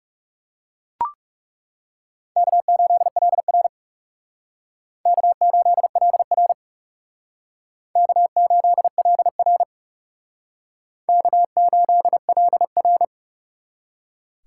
Morse code, heard as a keyed tone, about 38 wpm, 700 Hz. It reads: K8LR K8LR K8LR K8LR